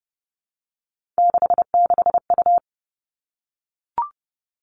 Morse code, heard as a keyed tone, about 30 wpm, 700 Hz. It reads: 66U
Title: Three Challenging Letter-Number Combinations 30wpm